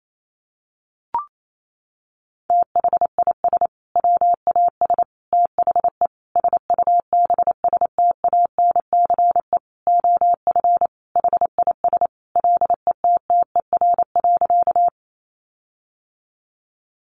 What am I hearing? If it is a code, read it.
Morse code, heard as a keyed tone, about 28 wpm, 700 Hz. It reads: THIS WAS THE SUBSTANCE OF HIS LETTER.
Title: Sentences from Top 1500 Words 28wpm